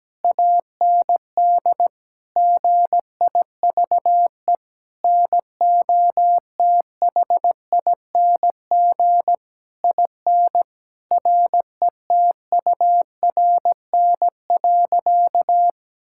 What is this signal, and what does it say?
Morse code, heard as a keyed tone, about 17 wpm, 700 Hz. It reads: AND GIVE NOTHING IN RETURN.
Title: Sentences from Top 1000 Words - Review Part 6 - 17wpm